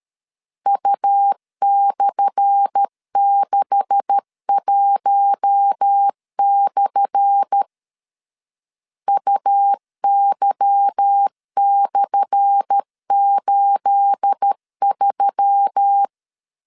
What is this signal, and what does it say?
Morse code, heard as a keyed tone, about 13 wpm, 800 Hz. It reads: U/61/ UY/83
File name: LEZIONI / 10 SIGNS / 10A SLASH 12wpm 2314